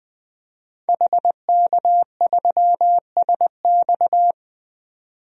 Morse code, heard as a keyed tone, about 20 wpm, 700 Hz. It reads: HK3SX
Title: DXCC-100 Call Signs - 5 Characters - 20wpm